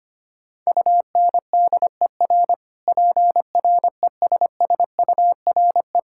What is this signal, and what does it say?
Morse code, heard as a keyed tone, about 25 wpm, 700 Hz. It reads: UNDER PRESSURE
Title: Sentences from Top 1000 Words - Review Part 6 - 25wpm